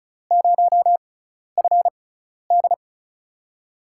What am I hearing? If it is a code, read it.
Morse code, heard as a keyed tone, about 35 wpm, spaced as for 12 wpm, 700 Hz. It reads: 0FD
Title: USA Call Signs - 5 Characters - 6x Character Spacing - 35wpm